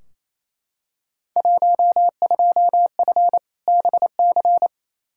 Morse code, heard as a keyed tone, about 28 wpm, 700 Hz. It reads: 12F BC
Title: Field Day Exchanges Encore - 28wpm